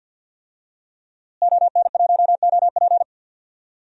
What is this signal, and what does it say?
Morse code, heard as a keyed tone, about 46 wpm, 700 Hz. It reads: ON1OP
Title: DXCC-100 Call Signs - 5 Characters - Repeated 2x - 50wpm